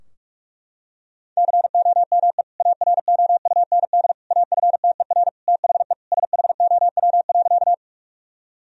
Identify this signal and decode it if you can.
Morse code, heard as a keyed tone, about 46 wpm, 700 Hz. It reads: COME AROUND AFTER THE SHOW.
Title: Sentences from Top 1000 Words - Review Part 3 - 45wpm